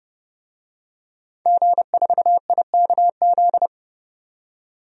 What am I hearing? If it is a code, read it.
Morse code, heard as a keyed tone, about 32 wpm, 700 Hz. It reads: G4IKZ
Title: DXCC-100 Call Signs - 5 Characters - 30wpm